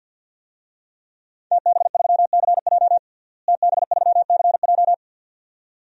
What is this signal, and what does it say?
Morse code, heard as a keyed tone, about 52 wpm, 700 Hz. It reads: TB3CJ TB3CJ
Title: DXCC-100 Call Signs - 5 Characters - Repeated 2x - 50wpm